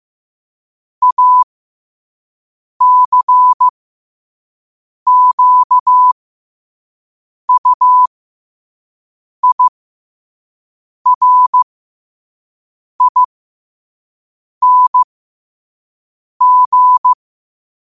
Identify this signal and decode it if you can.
Morse code, heard as a keyed tone, about 15 wpm, spaced as for 5 wpm, 1000 Hz. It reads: ACQUIRING
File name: qotdfiles / 5wpm